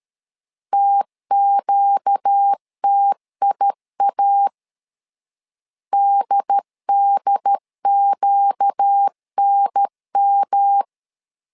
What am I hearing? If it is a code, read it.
Morse code, heard as a keyed tone, about 13 wpm, 800 Hz. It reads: TQTIA DDQNM
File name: LEZIONI / 4 Q 12wpm 1833